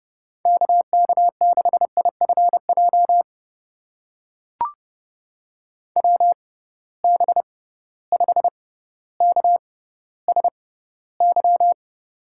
Morse code, heard as a keyed tone, about 30 wpm, spaced as for 15 wpm, 700 Hz. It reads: KK6IFJ WB5KSY
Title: USA Call Signs - 6 Characters - 6x Character Spacing - 30wpm